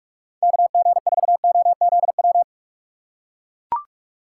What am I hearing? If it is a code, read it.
Morse code, heard as a keyed tone, about 42 wpm, 700 Hz. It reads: KG4OZW
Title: USA Call Signs - 6 Characters - 2x Character Spacing - 45wpm